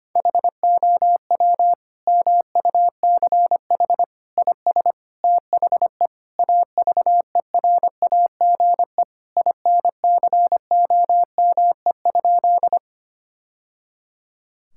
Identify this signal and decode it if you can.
Morse code, heard as a keyed tone, about 25 wpm, 700 Hz. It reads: HOW MUCH IS THE AVERAGE INCOME?
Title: Sentences from Top 1000 Words - Review Part 6 - 25wpm